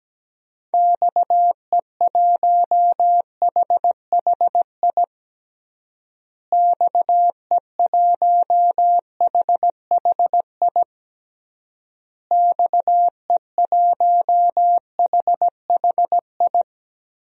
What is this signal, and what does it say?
Morse code, heard as a keyed tone, about 17 wpm, 700 Hz. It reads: XE1HHI XE1HHI XE1HHI
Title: DXCC-100 Call Signs - 6 Characters - Repeated 3x - 17wpm